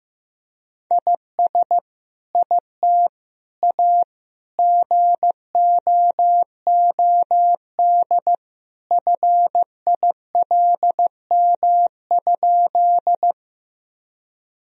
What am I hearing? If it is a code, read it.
Morse code, heard as a keyed tone, about 15 wpm, 700 Hz. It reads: IS IT A GOOD FILM?
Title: Sentences from Top 1000 Words - Review Part 3 - 15wpm